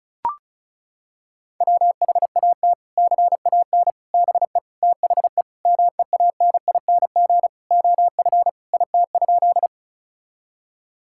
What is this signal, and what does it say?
Morse code, heard as a keyed tone, about 36 wpm, 700 Hz. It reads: WHAT CAN BE THE MEANING OF IT?
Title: Sentences from Top 1000 Words - Review Part 6 - 35wpm